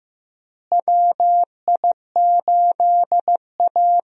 Morse code, heard as a keyed tone, about 15 wpm, 700 Hz. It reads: WI8A
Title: USA Call Signs - 4 Characters - Repeated 3x - 15wpm